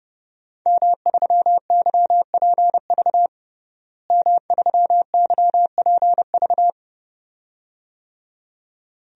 Morse code, heard as a keyed tone, about 30 wpm, 700 Hz. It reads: M3YPV M3YPV
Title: DXCC-100 Call Signs - 5 Characters - Repeated 2x - 30wpm